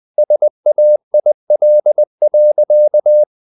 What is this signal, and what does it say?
Morse code, heard as a keyed tone, about 20 wpm, 600 Hz. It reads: SAIL.
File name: the old man and the sea part 2 20wpm